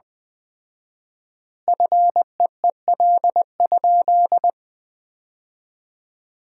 Morse code, heard as a keyed tone, about 20 wpm, 700 Hz. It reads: FEEL?
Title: 4x Word-Spacing - Sentences from Top 400 Words - 20wpm